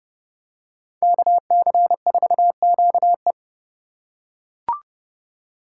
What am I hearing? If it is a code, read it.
Morse code, heard as a keyed tone, about 30 wpm, 700 Hz. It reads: KC4QE
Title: USA Call Signs - 3x Character Spacing - 30wpm